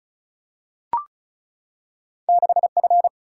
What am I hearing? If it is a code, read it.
Morse code, heard as a keyed tone, about 36 wpm, 700 Hz. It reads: 6F
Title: Single-Letter Single-Number 35wpm